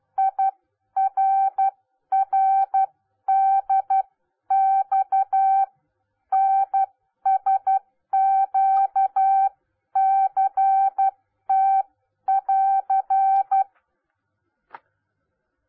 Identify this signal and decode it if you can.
Morse code, heard as a keyed tone, about 12 wpm, spaced as for 10 wpm, 780 Hz. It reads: IRRDX NSQCT<AR>